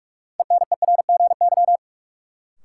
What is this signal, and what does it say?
Morse code, heard as a keyed tone, about 46 wpm, 700 Hz. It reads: ENERGY
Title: Sentences from Top 900 Words - New Word Drill - 45wpm